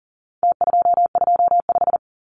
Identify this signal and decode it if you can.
Morse code, heard as a keyed tone, about 40 wpm, 700 Hz. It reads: T225
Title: Numbers - Contest Style - 4-Digit 40wpm